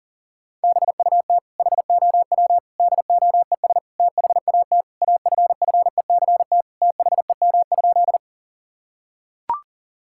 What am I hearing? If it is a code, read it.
Morse code, heard as a keyed tone, about 40 wpm, 700 Hz. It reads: BUT HOW DOES THAT AFFECT THEM?
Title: Sentences from Top 1000 Words - Review Part 6 - 40wpm